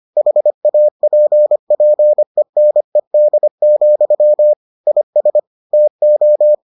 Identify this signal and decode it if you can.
Morse code, heard as a keyed tone, about 25 wpm, 600 Hz. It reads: HAPPENED, IS TO